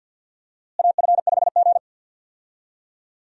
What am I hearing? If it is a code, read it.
Morse code, heard as a keyed tone, about 53 wpm, 700 Hz. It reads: AF5G
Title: USA Call Signs - 4 Characters - Repeated 2x - 50wpm